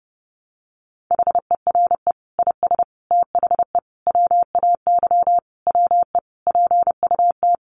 Morse code, heard as a keyed tone, about 30 wpm, 700 Hz. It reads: HERE IS THE WAY WE PUT
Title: Sentences from Top 200 Words 30wpm